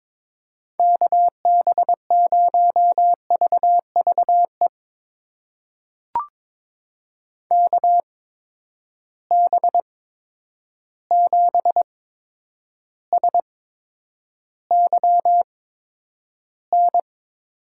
Morse code, heard as a keyed tone, about 22 wpm, spaced as for 9 wpm, 700 Hz. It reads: KB0VVE KB7SYN
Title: USA Call Signs - 6 Characters - 8x Character Spacing - 22wpm